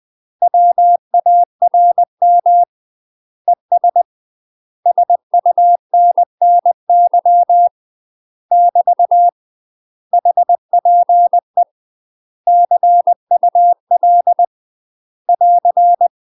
Morse code, heard as a keyed tone, about 20 wpm, 700 Hz. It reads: WARM ES SUNNY <BT> HPE CUL <AR>
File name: UK MASTER QSOs TRAINING / 20WPM / UK MASTER QSOs 0060